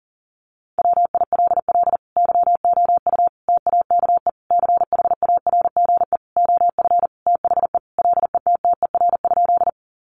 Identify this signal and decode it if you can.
Morse code, heard as a keyed tone, about 40 wpm, 700 Hz. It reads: WILL YOU TAKE CHARGE OF THE LETTER?
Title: Sentences from Top 900 Words 40wpm